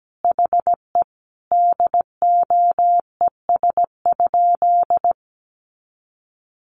Morse code, heard as a keyed tone, about 17 wpm, 700 Hz. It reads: HE DOES?
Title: Sentences from Top 1300 Words 17wpm